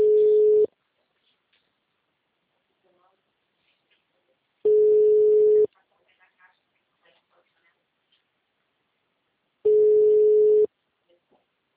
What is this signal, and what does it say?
Ringback tone (Europe, 425 Hz), with speech.